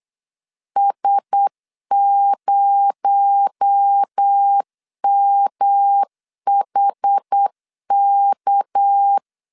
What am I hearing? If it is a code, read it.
Morse code, heard as a keyed tone, about 8 wpm, 800 Hz. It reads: S0MHK
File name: 8wpm